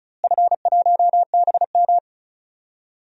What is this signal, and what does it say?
Morse code, heard as a keyed tone, about 36 wpm, 700 Hz. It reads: F1BM
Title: DXCC-100 Call Signs - 4 Characters - Repeated 3x - 35wpm